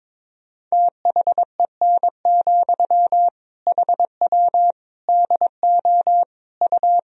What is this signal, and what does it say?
Morse code, heard as a keyed tone, about 22 wpm, 700 Hz. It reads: THEN, HW DO U
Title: Sentences from Top 100 Words and Abbreviations 22wpm